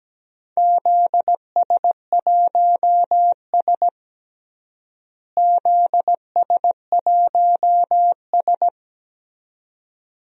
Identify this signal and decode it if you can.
Morse code, heard as a keyed tone, about 17 wpm, 700 Hz. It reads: ZS1S ZS1S
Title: DXCC-100 Call Signs - 4 Characters - Repeated 2x - 17wpm